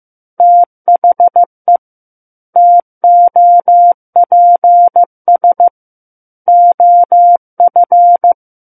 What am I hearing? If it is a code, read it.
Morse code, heard as a keyed tone, about 15 wpm, 700 Hz. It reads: THE TOPS OF
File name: Chapter 6 - 15 WPM